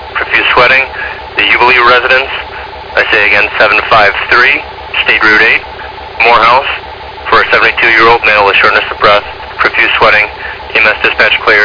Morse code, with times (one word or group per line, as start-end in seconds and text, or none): none